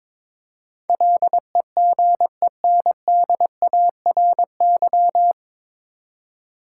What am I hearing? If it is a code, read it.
Morse code, heard as a keyed tone, about 22 wpm, 700 Hz. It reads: LEGENDARY
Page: Morse Code Ninja practice 22wpm